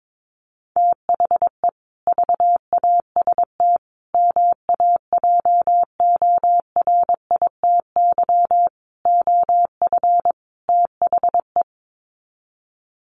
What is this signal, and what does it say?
Morse code, heard as a keyed tone, about 22 wpm, 700 Hz. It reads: THE VAST MAJORITY OF THE
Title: Sets of 5 Words Encore 22wpm